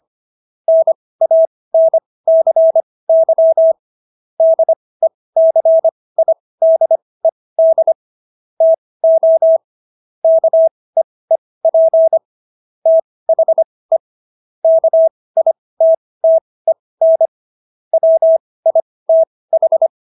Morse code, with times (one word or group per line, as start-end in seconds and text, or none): 0.7-3.7 NANCY
4.4-7.9 DECIDED
8.6-9.6 TO
10.2-12.2 KEEP
12.9-14.0 THE
14.6-17.3 KITTEN
17.9-19.9 WITH